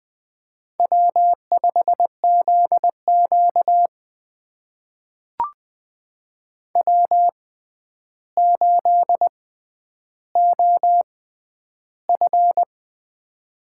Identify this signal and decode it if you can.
Morse code, heard as a keyed tone, about 20 wpm, spaced as for 10 wpm, 700 Hz. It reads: W5ZQ W8OF
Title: USA Call Signs - 4 Characters - 6x Character Spacing - 20wpm